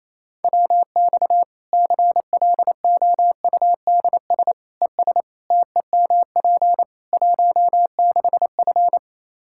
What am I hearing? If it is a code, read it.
Morse code, heard as a keyed tone, about 28 wpm, 700 Hz. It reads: WX CLOUDS ES TEMP 16F
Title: Practice QSO - Exchange 2 - Info - 28wpm